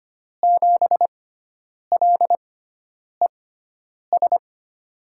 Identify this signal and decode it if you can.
Morse code, heard as a keyed tone, about 25 wpm, spaced as for 9 wpm, 700 Hz. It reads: 7LES
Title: USA Call Signs - 5 Characters - 6x Character Spacing - 25wpm